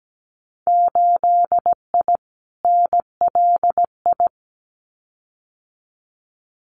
Morse code, heard as a keyed tone, about 17 wpm, 700 Hz. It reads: 8I NLI
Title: Winter Field Day Exchanges Encore 17wpm